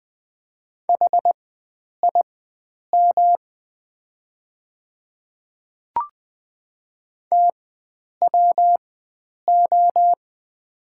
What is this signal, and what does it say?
Morse code, heard as a keyed tone, about 20 wpm, spaced as for 9 wpm, 700 Hz. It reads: HIM TWO
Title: 4x Character Spacing - Top 100 Words - 20wpm